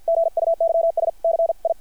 Morse code, heard as a keyed tone, about 50 wpm, 660 Hz. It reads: GUYS CI
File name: archived / CRSnet